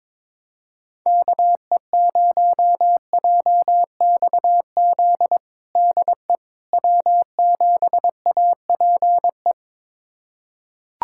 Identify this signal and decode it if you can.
Morse code, heard as a keyed tone, about 22 wpm, 700 Hz. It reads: KE0JXZ DE W7APE
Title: USA Callsign DE Callsign 22wpm